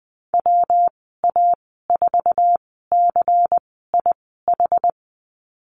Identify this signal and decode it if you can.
Morse code, heard as a keyed tone, about 19 wpm, spaced as for 14 wpm, 700 Hz. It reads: WA4CIH
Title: 140 Lesson 23g - Letter f - Call Signs 20wpm